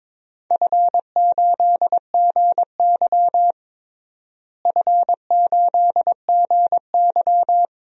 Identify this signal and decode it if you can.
Morse code, heard as a keyed tone, about 22 wpm, 700 Hz. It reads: F8GY F8GY
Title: DXCC-100 Call Signs - 4 Characters - Repeated 2x - 22wpm